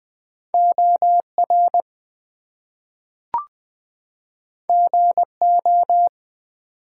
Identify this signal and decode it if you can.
Morse code, heard as a keyed tone, about 20 wpm, 700 Hz. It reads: OR GO
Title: Two Letter Words - 20wpm